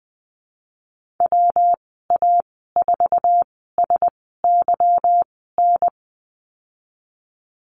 Morse code, heard as a keyed tone, about 20 wpm, spaced as for 15 wpm, 700 Hz. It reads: WA4SYN
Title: 154 Lesson 25g - Letter p - Call Signs 20wpm